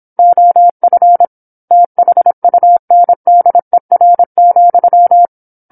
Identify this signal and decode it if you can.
Morse code, heard as a keyed tone, about 27 wpm, 700 Hz. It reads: OF THUNDER,
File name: 08-Chapter-08 26 WPM